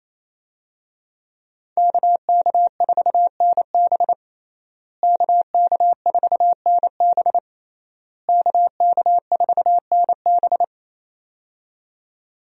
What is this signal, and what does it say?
Morse code, heard as a keyed tone, about 28 wpm, 700 Hz. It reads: KK4NB KK4NB KK4NB
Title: USA Call Signs - 5 Characters - Repeated 3x - 28wpm